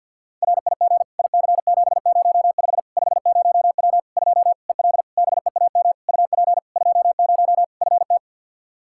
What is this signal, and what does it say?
Morse code, heard as a keyed tone, about 53 wpm, 700 Hz. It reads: RIG IC705 50W 2 EL BEAM UP 20 FT
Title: Practice QSO - Exchange 2 - Info - 50wpm